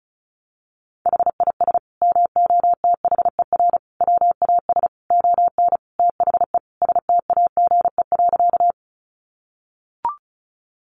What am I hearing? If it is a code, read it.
Morse code, heard as a keyed tone, about 34 wpm, 700 Hz. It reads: HIS MOTHER WAS ON THE STAGE.
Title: Sentences from Top 1200 Words 35wpm